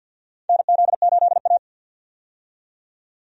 Morse code, heard as a keyed tone, about 48 wpm, 700 Hz. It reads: NZ8A
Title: USA Call Signs - 4 Characters - 50wpm